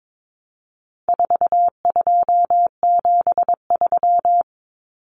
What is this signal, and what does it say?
Morse code, heard as a keyed tone, about 22 wpm, 700 Hz. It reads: 4273
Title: Numbers - 4-Digit 22wpm